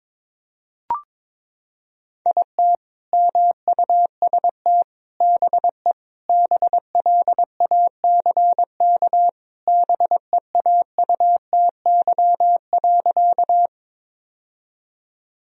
Morse code, heard as a keyed tone, about 22 wpm, 700 Hz. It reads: IT MUST BE BLACK BEAUTY.
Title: Sentences from Top 1000 Words - Review Part 2 - 22wpm